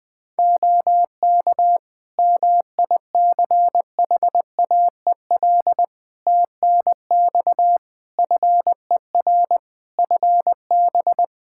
Morse code, heard as a keyed tone, about 20 wpm, 700 Hz. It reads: OK MICHAEL TNX FER FB
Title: Practice QSO - Exchange 4 - Ending - 20wpm